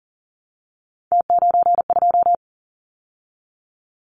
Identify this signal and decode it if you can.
Morse code, heard as a keyed tone, about 38 wpm, 700 Hz. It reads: T92
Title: Callsign Prefixes for Top 100 DXCC - 40wpm